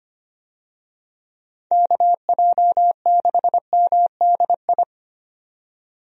Morse code, heard as a keyed tone, about 25 wpm, 700 Hz. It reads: KJ6MDI
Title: USA Call Signs - 6 Characters - 25wpm